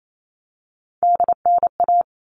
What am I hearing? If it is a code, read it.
Morse code, heard as a keyed tone, about 28 wpm, 700 Hz. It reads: DNA
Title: All Cut Numbers - 3-Digit 28wpm